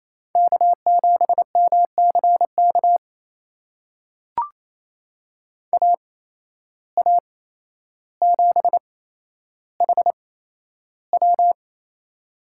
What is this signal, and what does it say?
Morse code, heard as a keyed tone, about 28 wpm, spaced as for 12 wpm, 700 Hz. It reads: K7MCK AA7HW
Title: USA Call Signs - 5 Characters - 8x Character Spacing - 28wpm